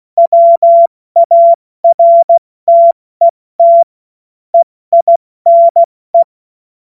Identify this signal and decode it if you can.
Morse code, heard as a keyed tone, about 16 wpm, spaced as for 14 wpm, 680 Hz. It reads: WARTET EINE
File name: Morsekurs / Witze / Witz 09 April 2024 70 F80